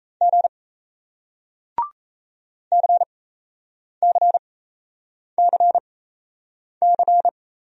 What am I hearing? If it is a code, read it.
Morse code, heard as a keyed tone, about 38 wpm, spaced as for 8 wpm, 700 Hz. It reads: G CCCC